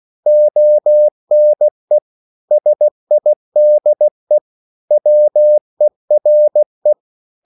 Morse code, heard as a keyed tone, about 16 wpm, 600 Hz. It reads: ONE SIDE WERE